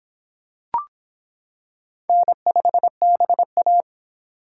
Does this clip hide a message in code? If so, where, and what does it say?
Yes: Morse code from 2.1 to 3.8 s, heard as a keyed tone, about 25 wpm, 700 Hz. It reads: N5BA